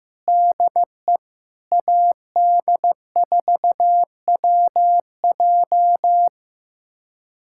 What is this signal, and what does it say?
Morse code, heard as a keyed tone, about 15 wpm, 700 Hz. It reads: DE AD4WJ
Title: USA Callsign DE Callsign 15wpm